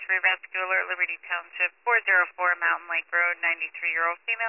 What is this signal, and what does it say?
Emergency Alert System (SAME) tones, heard as harsh buzzing bursts, with speech. It is heard from 0.7 to 3.6 s.